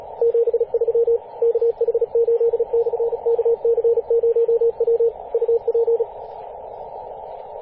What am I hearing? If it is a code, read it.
Morse code, heard as a keyed tone, about 34 wpm, 460 Hz, with radio noise. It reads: 73 KH8/KC0W UP
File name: posts / 2016-apr-22-2153z-kh8-kc0w-18-080mhz-cw